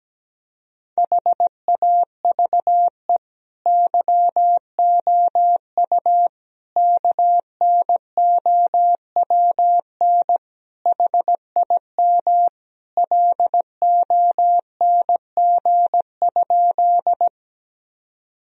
Morse code, heard as a keyed tone, about 17 wpm, 700 Hz. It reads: HAVE YOU KNOWN HIM LONG?
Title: Sentences from Top 1000 Words - Review Part 3 - 17wpm